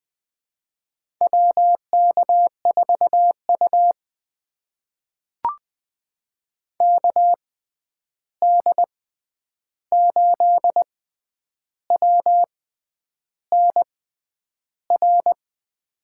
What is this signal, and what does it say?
Morse code, heard as a keyed tone, about 20 wpm, spaced as for 9 wpm, 700 Hz. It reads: WK4U KD8WNR
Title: USA Call Signs - 6x Character Spacing - 20wpm